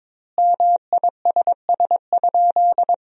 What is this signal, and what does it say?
Morse code, heard as a keyed tone, about 21 wpm, 700 Hz. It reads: MISS?
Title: 2x Word-Spacing - Sentences from Top 500 Words - 22wpm